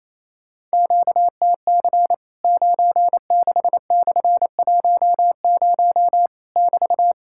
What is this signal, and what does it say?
Morse code, heard as a keyed tone, about 28 wpm, 700 Hz. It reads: QTC 96/10 <BT>
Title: Worked All Europe DX Contest - QTC Exchange - 28wpm